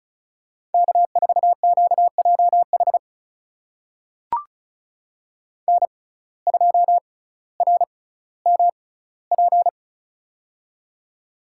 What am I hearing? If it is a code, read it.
Morse code, heard as a keyed tone, about 36 wpm, spaced as for 18 wpm, 700 Hz. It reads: K4QJH N2RMP
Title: USA Call Signs - 5 Characters - 6x Character Spacing - 35wpm